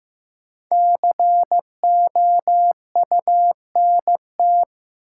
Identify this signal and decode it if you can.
Morse code, heard as a keyed tone, about 16 wpm, 700 Hz. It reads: COUNT
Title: Sentences from Top 1500 Words - New Word Drill Alt - 15wpm